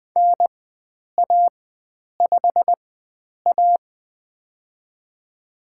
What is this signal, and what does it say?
Morse code, heard as a keyed tone, about 20 wpm, spaced as for 9 wpm, 700 Hz. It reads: NA5A
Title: USA Call Signs - 4 Characters - 4x Character Spacing - 20wpm